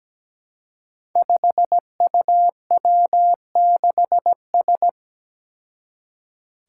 Morse code, heard as a keyed tone, about 17 wpm, 700 Hz. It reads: 5UW6S
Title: Five Challenging Letter-Number Combinations 17wpm